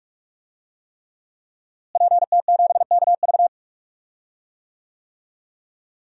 Morse code, heard as a keyed tone, about 41 wpm, 700 Hz. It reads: PT7KV
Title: DXCC-100 Call Signs - 5 Characters - Repeated 3x - 45wpm